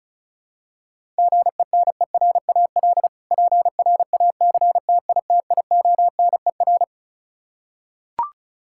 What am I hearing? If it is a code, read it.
Morse code, heard as a keyed tone, about 34 wpm, 700 Hz. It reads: GENERAL PRACTITIONER
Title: K1USN SST QSO Practice 35wpm